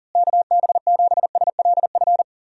Morse code, heard as a keyed tone, about 40 wpm, 700 Hz. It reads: KB7SLF
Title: USA Call Signs - 6 Characters - Repeated 2x - 40wpm